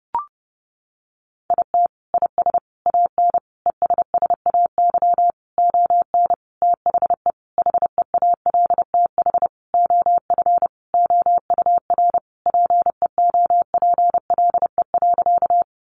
Morse code, heard as a keyed tone, about 28 wpm, 700 Hz. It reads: IT IS AN ESSAY ON THE HEALTH OF OUR PEOPLE.
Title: Sentences from Top 1500 Words 30wpm